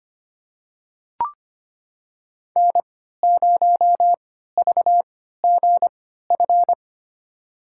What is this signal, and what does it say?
Morse code, heard as a keyed tone, about 25 wpm, spaced as for 14 wpm, 700 Hz. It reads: N0VGF
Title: USA Call Signs - 5 Characters - 3x Character Spacing - 25wpm